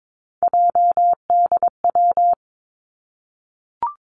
Morse code, heard as a keyed tone, about 22 wpm, 700 Hz. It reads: JDW
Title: Three Challenging Letter-Number Combinations 22wpm